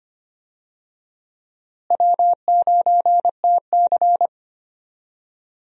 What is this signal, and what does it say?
Morse code, heard as a keyed tone, about 25 wpm, 700 Hz. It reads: W9TC
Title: USA Call Signs - 4 Characters - Repeated 2x - 25wpm